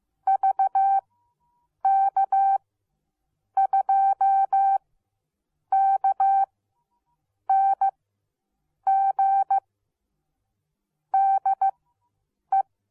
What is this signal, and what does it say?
Morse code, heard as a keyed tone, about 15 wpm, spaced as for 7 wpm, 780 Hz. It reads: VK2KNG DE